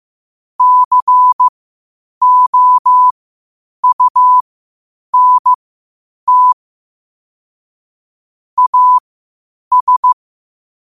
Morse code, heard as a keyed tone, about 15 wpm, spaced as for 8 wpm, 1000 Hz. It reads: COUNT AS
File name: qotdfiles / 8wpm